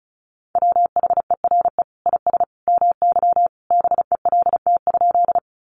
Morse code, heard as a keyed tone, about 34 wpm, 700 Hz. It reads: WHERE IS MY BELT?
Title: Sentences from Top 1400 Words 35wpm